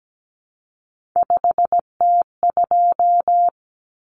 Morse code, heard as a keyed tone, about 17 wpm, 700 Hz. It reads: 5T2
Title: Common Cut Numbers - 3-Digit 17wpm